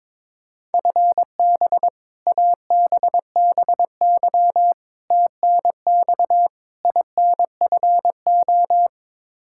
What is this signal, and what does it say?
Morse code, heard as a keyed tone, about 22 wpm, 700 Hz. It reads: FB ABBY TNX INFO
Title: Practice QSO - Exchange 3 - More Info - 22wpm